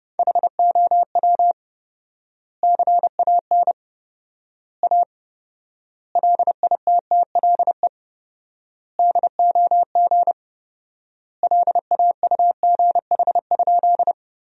Morse code, heard as a keyed tone, about 30 wpm, 700 Hz. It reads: HOW CAN A LITTLE DOG LAUGH?
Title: 4x Word-Spacing - Sentences from Top 500 Words - 30wpm